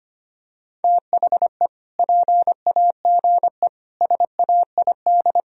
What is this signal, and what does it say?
Morse code, heard as a keyed tone, about 25 wpm, 700 Hz. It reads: THE PAGE SAID
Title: Sentences from Top 1000 Words - Review Part 1 - 25wpm